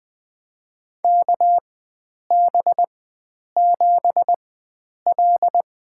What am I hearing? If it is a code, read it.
Morse code, heard as a keyed tone, about 20 wpm, spaced as for 9 wpm, 700 Hz. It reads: KB7L
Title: USA Call Signs - 4x Character Spacing - 20wpm